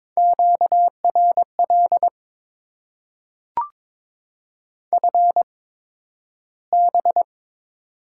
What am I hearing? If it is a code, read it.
Morse code, heard as a keyed tone, about 23 wpm, spaced as for 12 wpm, 700 Hz. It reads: QRL FB
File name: Top 20 QSO Elements - 8x Character Spacing - 22wpm